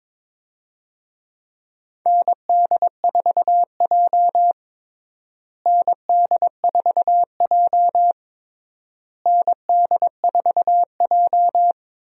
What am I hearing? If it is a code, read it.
Morse code, heard as a keyed tone, about 22 wpm, 700 Hz. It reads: ND4J ND4J ND4J
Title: USA Call Signs - 4 Characters - Repeated 3x - 22wpm